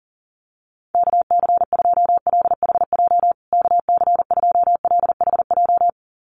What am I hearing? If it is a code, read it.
Morse code, heard as a keyed tone, about 40 wpm, 700 Hz. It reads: KC2LHJ KC2LHJ
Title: USA Call Signs Repeated 2x 40wpm